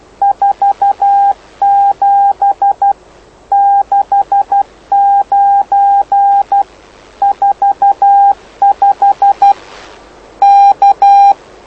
Morse code, heard as a keyed tone, about 12 wpm, 800 Hz. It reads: 47 69 45 K
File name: lu4aao / lu7aa 09dic2011 exp planeador vuelo1